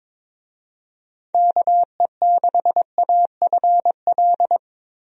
Morse code, heard as a keyed tone, about 22 wpm, 700 Hz. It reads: KE6AFL
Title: USA Call Signs - 6 Characters - 22wpm